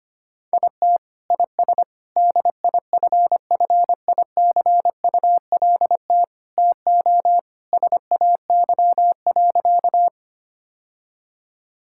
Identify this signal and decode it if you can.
Morse code, heard as a keyed tone, about 25 wpm, 700 Hz. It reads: IT IS DIFFICULT TO SAY.
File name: practice / 25wpm / Sentences with 4-Word Phrases - 25wpm